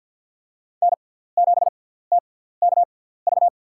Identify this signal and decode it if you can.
Morse code, heard as a keyed tone, about 48 wpm, spaced as for 17 wpm, 700 Hz. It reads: N7TKV
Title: USA Call Signs - 5 Characters - 6x Character Spacing - 50wpm